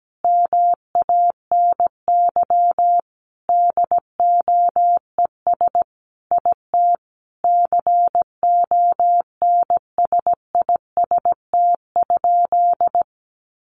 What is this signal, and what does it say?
Morse code, heard as a keyed tone, about 17 wpm, 700 Hz. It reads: MANY DOES IT CONSIST?
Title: Sentences from Top 1400 Words 17wpm